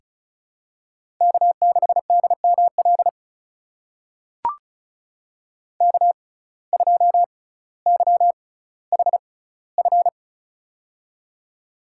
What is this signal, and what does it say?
Morse code, heard as a keyed tone, about 36 wpm, spaced as for 18 wpm, 700 Hz. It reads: K6DML K2YHF
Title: USA Call Signs - 5 Characters - 6x Character Spacing - 35wpm